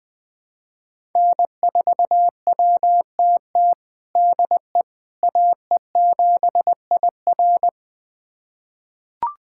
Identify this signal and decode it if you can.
Morse code, heard as a keyed tone, about 20 wpm, 700 Hz. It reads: N4WTT DE AE7IR
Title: USA Callsign DE Callsign - 5 Characters - 20wpm